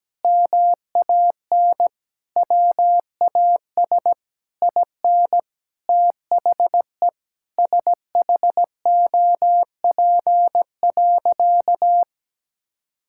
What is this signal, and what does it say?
Morse code, heard as a keyed tone, about 17 wpm, 700 Hz. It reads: MAN WAS IN THE SHOP.
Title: Sentences from Top 1600 Words 17wpm